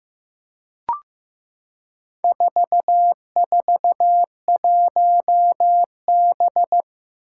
Morse code, heard as a keyed tone, about 16 wpm, 700 Hz. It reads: 441B
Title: Four Challenging Letter-Number Combinations 15wpm